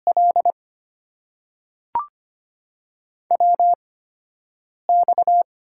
Morse code, heard as a keyed tone, about 25 wpm, spaced as for 7 wpm, 700 Hz. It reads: L WX
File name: Top 90 QSO Elements - 8x Character Spacing - 25wpm